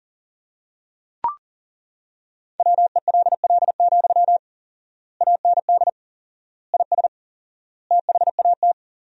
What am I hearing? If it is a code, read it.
Morse code, heard as a keyed tone, about 37 wpm, 700 Hz. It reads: WELL, AND IS THAT
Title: 4x Word-Spacing - Sentences from Top 200 Words - 40wpm